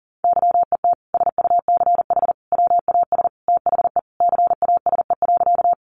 Morse code, heard as a keyed tone, about 40 wpm, 700 Hz. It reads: YET SUCH WAS THE CASE.
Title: Sentences from Top 700 Words 40wpm